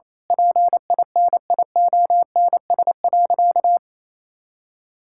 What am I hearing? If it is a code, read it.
Morse code, heard as a keyed tone, about 27 wpm, 700 Hz. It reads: PINIONS.
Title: Pangram Challenge v2 - 28wpm